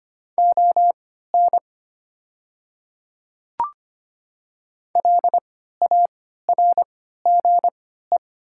Morse code, heard as a keyed tone, about 25 wpm, spaced as for 14 wpm, 700 Hz. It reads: ON LARGE
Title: 3x Character Spacing - Top 200 Words - 25wpm